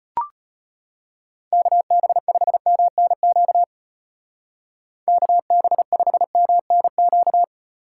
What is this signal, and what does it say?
Morse code, heard as a keyed tone, about 34 wpm, 700 Hz. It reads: KB5MNQ KB5MNQ